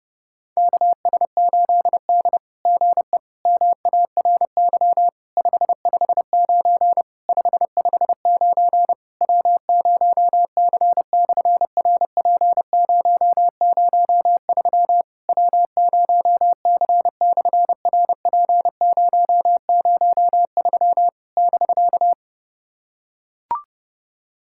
Morse code, heard as a keyed tone, about 32 wpm, 700 Hz. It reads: KS8D GE MARY 559 559 W0C/RP003 W0C/RP003 <BK>
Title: SOTA QSO Practice 30wpm